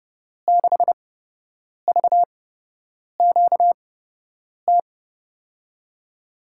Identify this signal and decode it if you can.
Morse code, heard as a keyed tone, about 30 wpm, spaced as for 8 wpm, 700 Hz. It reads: 6VQT
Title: USA Call Signs - 6 Characters - 8x Character Spacing - 30wpm